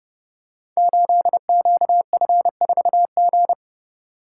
Morse code, heard as a keyed tone, about 30 wpm, 700 Hz. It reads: 8QF4G
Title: Code Groups - 5 Letter:Numbers - 30wpm